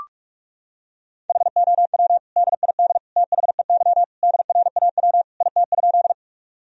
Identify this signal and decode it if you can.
Morse code, heard as a keyed tone, about 41 wpm, 700 Hz. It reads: HOW DID THEY DRAW IT?
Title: Sentences from Top 1000 Words - Review Part 1 - 45wpm